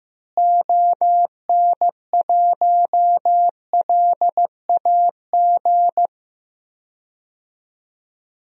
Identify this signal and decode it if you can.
Morse code, heard as a keyed tone, about 15 wpm, 700 Hz. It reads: ON1LAG
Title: DXCC-100 Call Signs - 6 Characters - Repeated 3x - 15wpm